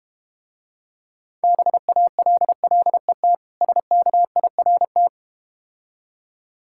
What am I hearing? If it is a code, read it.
Morse code, heard as a keyed tone, about 32 wpm, 700 Hz. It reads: BALLET SKIRT